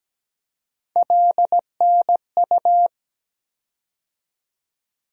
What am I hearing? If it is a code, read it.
Morse code, heard as a keyed tone, about 17 wpm, 700 Hz. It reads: LNU
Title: USA Call Sign Postfixes - 3 Letters 17wpm